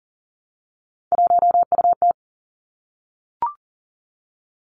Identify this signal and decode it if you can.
Morse code, heard as a keyed tone, about 39 wpm, 700 Hz. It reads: 1UT
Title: Three Challenging Letter-Number Combinations 40wpm